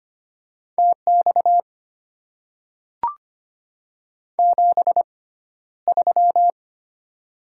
Morse code, heard as a keyed tone, about 25 wpm, spaced as for 13 wpm, 700 Hz. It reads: TX 73
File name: Top 90 QSO Elements - 6x Character Spacing - 25wpm